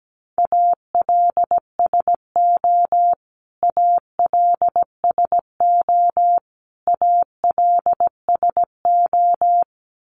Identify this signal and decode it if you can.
Morse code, heard as a keyed tone, about 17 wpm, 700 Hz. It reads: ALSO ALSO ALSO
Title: Top 500 Words Repeated 17wpm